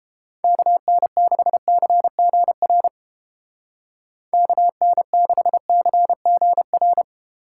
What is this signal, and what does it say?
Morse code, heard as a keyed tone, about 32 wpm, 700 Hz. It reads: KN6CGR KN6CGR